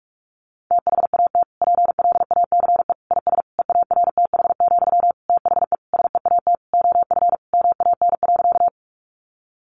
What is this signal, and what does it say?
Morse code, heard as a keyed tone, about 44 wpm, 700 Hz. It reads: THAT PLACE IS EARTH, THE SEAT OF MAN.
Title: Sentences from Top 1500 Words 45wpm